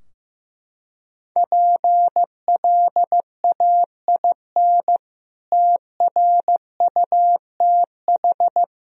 Morse code, heard as a keyed tone, about 15 wpm, 700 Hz. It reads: PLAIN TRUTH
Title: Sentences from Top 1000 Words - Review Part 2 - 15wpm